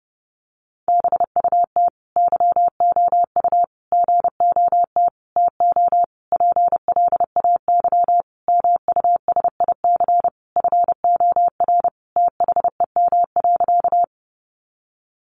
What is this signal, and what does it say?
Morse code, heard as a keyed tone, about 30 wpm, 700 Hz. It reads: BUT YOU GOT TO PLAY MUSIC FOR THEM.
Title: Sentences from Top 300 Words 30wpm